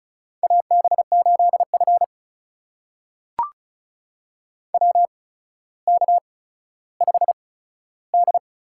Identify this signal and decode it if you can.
Morse code, heard as a keyed tone, about 36 wpm, spaced as for 15 wpm, 700 Hz. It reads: AB8F WK5D
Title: USA Call Signs - 4 Characters - 8x Character Spacing - 35wpm